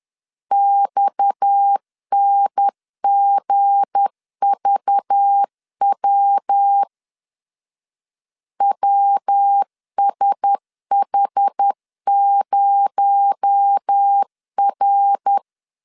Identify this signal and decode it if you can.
Morse code, heard as a keyed tone, about 10 wpm, 800 Hz. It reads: XNGVW WSH0R
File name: LEZIONI / 7 GF9Z / 7 G 10wpm 2501